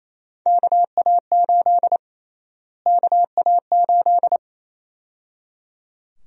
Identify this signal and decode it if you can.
Morse code, heard as a keyed tone, about 28 wpm, 700 Hz. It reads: KA8 KA8
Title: USA Call Sign Prefixes - Repeated 2x - 28wpm